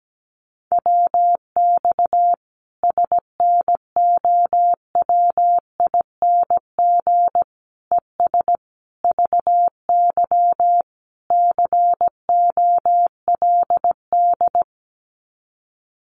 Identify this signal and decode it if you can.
Morse code, heard as a keyed tone, about 17 wpm, 700 Hz. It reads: WX SNOWING ES VY COLD
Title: Common QSO Phrases 17wpm